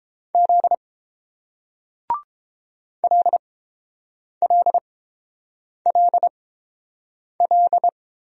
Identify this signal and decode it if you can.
Morse code, heard as a keyed tone, about 29 wpm, spaced as for 7 wpm, 690 Hz. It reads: Z LLLL